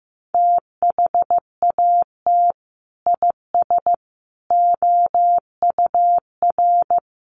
Morse code, heard as a keyed tone, about 15 wpm, 700 Hz. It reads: THAT IS OUR
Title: Sentences from Top 1500 Words 15wpm